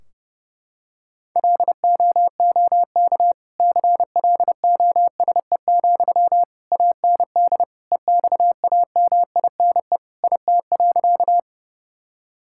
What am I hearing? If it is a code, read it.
Morse code, heard as a keyed tone, about 32 wpm, 700 Hz. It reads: LOOK CLOSE, AND EXAMINE IT.
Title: Sentences from Top 1600 Words 30wpm